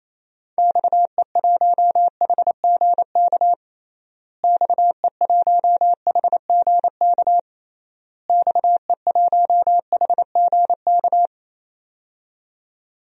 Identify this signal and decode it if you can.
Morse code, heard as a keyed tone, about 28 wpm, 700 Hz. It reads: XE1HGK XE1HGK XE1HGK
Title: DXCC-100 Call Signs - 6 Characters - Repeated 3x - 28wpm